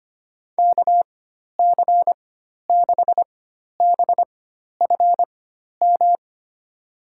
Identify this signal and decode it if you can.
Morse code, heard as a keyed tone, about 25 wpm, spaced as for 12 wpm, 700 Hz. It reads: KC6BFM
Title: USA Call Signs - 6 Characters - 4x Character Spacing - 25wpm